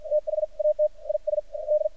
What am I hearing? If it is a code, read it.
Morse code, heard as a keyed tone, about 52 wpm, spaced as for 43 wpm, 610 Hz. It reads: THAT IS D
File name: archived / CRSnet